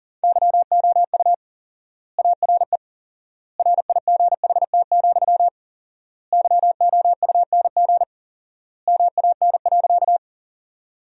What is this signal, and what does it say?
Morse code, heard as a keyed tone, about 40 wpm, 700 Hz. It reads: YOU ARE RIGHT, YOUNG MAN.
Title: 4x Word-Spacing - Sentences from Top 400 Words - 40wpm